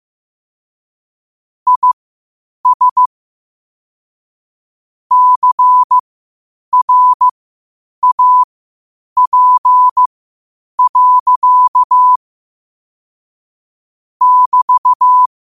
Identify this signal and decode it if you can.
Morse code, heard as a keyed tone, about 15 wpm, spaced as for 8 wpm, 1000 Hz. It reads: IS CRAP. <BT>
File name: qotdfiles / 8wpm